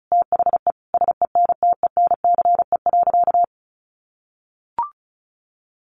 Morse code, heard as a keyed tone, about 34 wpm, 700 Hz. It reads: THE SENTENCE.
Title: Sentences from Top 1200 Words 35wpm